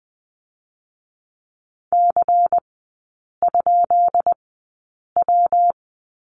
Morse code, heard as a keyed tone, about 20 wpm, 700 Hz. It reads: C ? W
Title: 218 Lesson 35e - Number 0 - Three Characters 20wpm